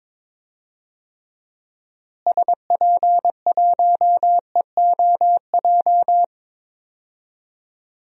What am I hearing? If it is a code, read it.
Morse code, heard as a keyed tone, about 22 wpm, 700 Hz. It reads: SP1EOJ
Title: DXCC-100 Call Signs - 6 Characters - Repeated 3x - 22wpm